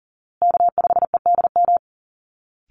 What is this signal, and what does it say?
Morse code, heard as a keyed tone, about 40 wpm, 700 Hz. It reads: K5EDM
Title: Call Signs Repeated 3x - 40wpm